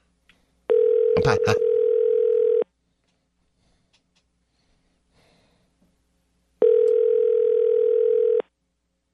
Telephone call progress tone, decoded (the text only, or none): ringback tone